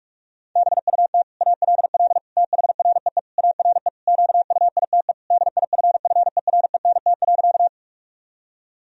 Morse code, heard as a keyed tone, about 46 wpm, 700 Hz. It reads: BUT ALL THREE ARE QUITE DIFFERENT.
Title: Sentences from Top 1000 Words - Review Part 3 - 45wpm